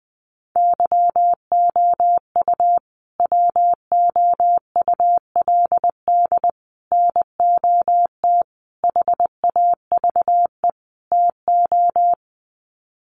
Morse code, heard as a keyed tone, about 20 wpm, 700 Hz. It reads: YOU WOULD NOT HAVE TO
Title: Sets of 5 Words Encore 20wpm